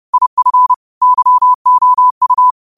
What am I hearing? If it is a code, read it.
Morse code, heard as a keyed tone, about 30 wpm, 1000 Hz. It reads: IF YOU
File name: qotdfiles / 30wpm